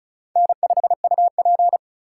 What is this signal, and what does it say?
Morse code, heard as a keyed tone, about 34 wpm, 700 Hz. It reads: N5UP
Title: USA Call Signs - 2x Character Spacing - 35wpm